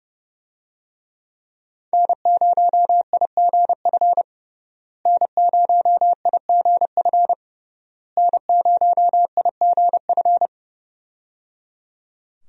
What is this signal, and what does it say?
Morse code, heard as a keyed tone, about 30 wpm, 700 Hz. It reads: N0IGF N0IGF N0IGF
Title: USA Call Signs - 5 Characters - Repeated 3x - 30wpm